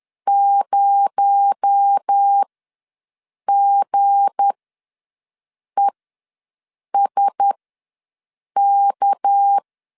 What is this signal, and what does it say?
Morse code, heard as a keyed tone, about 10 wpm, spaced as for 6 wpm, 800 Hz. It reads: 0GESK
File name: LEZIONI / 7 GF9Z / 7 G ws 10wpm 3843